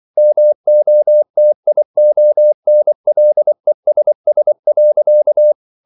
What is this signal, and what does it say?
Morse code, heard as a keyed tone, about 24 wpm, 600 Hz. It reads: MOTIONLESS.